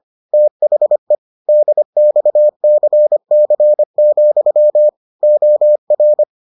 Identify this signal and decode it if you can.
Morse code, heard as a keyed tone, about 25 wpm, 600 Hz. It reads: THE DXCC, OR